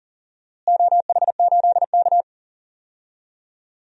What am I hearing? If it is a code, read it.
Morse code, heard as a keyed tone, about 40 wpm, 700 Hz. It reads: OH8K
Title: DXCC-100 Call Signs - 4 Characters - Repeated 3x - 40wpm